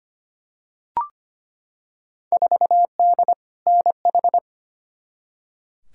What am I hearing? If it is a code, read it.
Morse code, heard as a keyed tone, about 25 wpm, 700 Hz. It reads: 4D NH
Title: Field Day Exchanges Encore - 25wpm